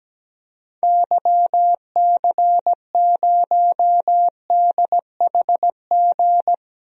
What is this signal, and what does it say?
Morse code, heard as a keyed tone, about 17 wpm, 700 Hz. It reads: YC0DHG
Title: DXCC-100 Call Signs - 6 Characters - 17wpm